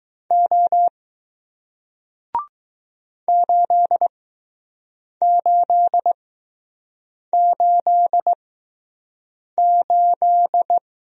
Morse code, heard as a keyed tone, about 20 wpm, spaced as for 6 wpm, 700 Hz. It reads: O 8888